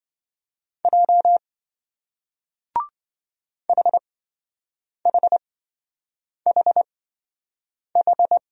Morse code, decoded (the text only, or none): J HHHH